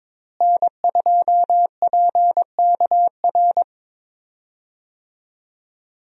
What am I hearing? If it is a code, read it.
Morse code, heard as a keyed tone, about 22 wpm, 700 Hz. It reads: N2PKR
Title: USA Call Signs - 5 Characters - Repeated 3x - 22wpm